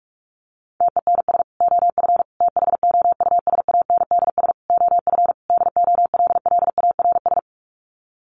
Morse code, heard as a keyed tone, about 44 wpm, 700 Hz. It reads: TENS OF THOUSANDS OF DOLLARS
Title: Sets of 5 Words Encore2 45wpm